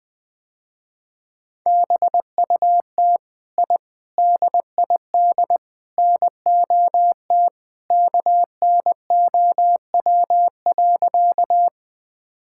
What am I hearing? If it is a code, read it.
Morse code, heard as a keyed tone, about 20 wpm, 700 Hz. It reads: BUT I DID NOT KNOW.
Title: Sentences with 4-Word Phrases - Spoken Beginning - 20wpm